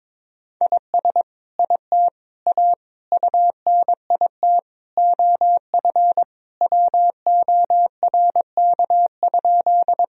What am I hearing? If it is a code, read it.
Morse code, heard as a keyed tone, about 22 wpm, 700 Hz. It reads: IS IT A UNIT OF WORK?
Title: Sentences from Top 1000 Words - Review Part 2 - 22wpm